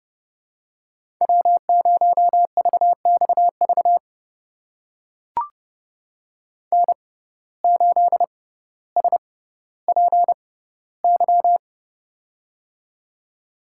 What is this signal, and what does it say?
Morse code, heard as a keyed tone, about 30 wpm, spaced as for 15 wpm, 700 Hz. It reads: W0VXV N8SPY
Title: USA Call Signs - 5 Characters - 6x Character Spacing - 30wpm